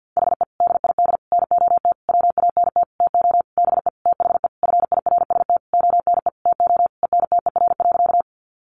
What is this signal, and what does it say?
Morse code, heard as a keyed tone, about 50 wpm, 700 Hz. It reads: HE DID NOT WANT TO BE THE FIRST ONE TO ENTER.
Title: Sentences from Top 1300 Words 50wpm